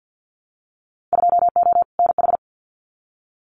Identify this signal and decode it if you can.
Morse code, heard as a keyed tone, about 50 wpm, 700 Hz. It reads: 2O NH
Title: Winter Field Day Exchanges Encore 50wpm